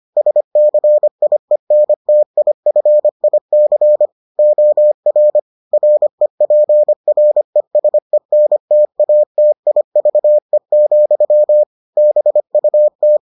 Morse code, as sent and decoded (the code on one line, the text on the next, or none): ... -.-. .. -. - .. ..-. .. -.-. / --- .-. / .-. .--. .-. ... -. - .- - .. ...- . --..-- / -... ..- -
SCIENTIFIC OR REPRESENTATIVE, BUT